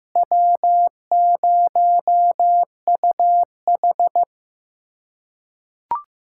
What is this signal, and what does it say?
Morse code, heard as a keyed tone, about 15 wpm, 700 Hz. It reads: W0UH
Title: USA Callsign DE Callsign - 4 Characters - 15wpm